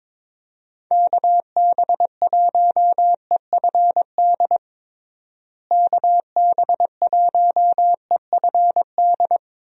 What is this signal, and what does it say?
Morse code, heard as a keyed tone, about 22 wpm, 700 Hz. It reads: KB1EFD KB1EFD
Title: USA Call Signs - 6 Characters - Repeated 2x - 22wpm